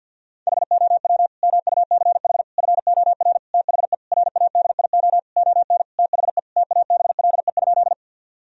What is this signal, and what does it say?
Morse code, heard as a keyed tone, about 50 wpm, 700 Hz. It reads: HOW MUCH FOR THE RADIO ON THE TABLE?
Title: Sentences from Top 1000 Words - Review Part 3 - 50wpm